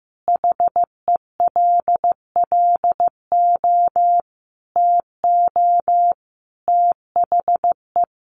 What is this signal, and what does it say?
Morse code, heard as a keyed tone, about 15 wpm, 700 Hz. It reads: HELLO TO THE
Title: Additional Easy Phrases 15wpm